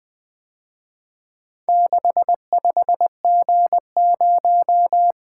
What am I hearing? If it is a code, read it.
Morse code, heard as a keyed tone, about 23 wpm, 700 Hz. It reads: 65G0
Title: Speed-Racing - Four Letter-Number Combinations - 20wpm